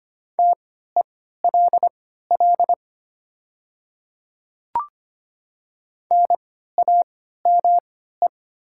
Morse code, heard as a keyed tone, about 25 wpm, spaced as for 14 wpm, 700 Hz. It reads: TELL NAME
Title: 3x Character Spacing - Top 200 Words - 25wpm